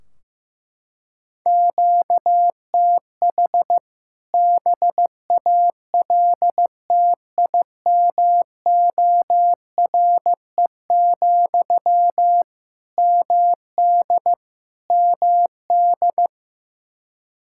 Morse code, heard as a keyed tone, about 16 wpm, 700 Hz. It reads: QTH BALTIMORE, MD MD
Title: K1USN SST QSO Practice 15wpm